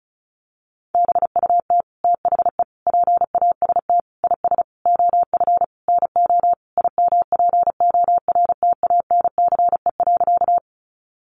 Morse code, heard as a keyed tone, about 34 wpm, 700 Hz. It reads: BUT THE PAST IS OF NO IMPORTANCE.
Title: Sentences from Top 1200 Words 35wpm